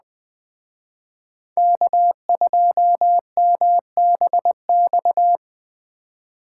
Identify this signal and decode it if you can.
Morse code, heard as a keyed tone, about 21 wpm, 700 Hz. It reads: K2MBX